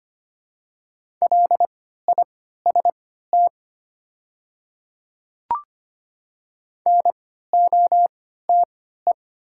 Morse code, heard as a keyed tone, about 25 wpm, spaced as for 14 wpm, 700 Hz. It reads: LIST NOTE
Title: 3x Character Spacing - Top 500 Words - 25wpm